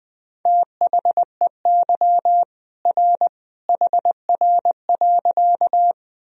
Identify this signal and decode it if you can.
Morse code, heard as a keyed tone, about 20 wpm, 700 Hz. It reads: THEY R HR.
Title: Sentences from Top 200 Words and Abbreviations 20wpm